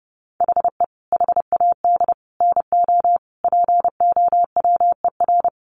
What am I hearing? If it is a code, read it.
Morse code, heard as a keyed tone, about 30 wpm, 700 Hz. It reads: HE HAD NO POWER
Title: Sentences from Top 1400 Words 30wpm